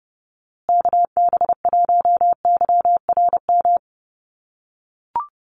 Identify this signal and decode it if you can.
Morse code, heard as a keyed tone, about 30 wpm, 700 Hz. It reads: KB1YRM
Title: USA Call Signs Repeated 2x 30wpm